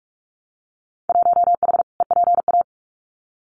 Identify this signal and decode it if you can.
Morse code, heard as a keyed tone, about 46 wpm, 700 Hz. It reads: 1H EPA
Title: Winter Field Day Exchanges 45wpm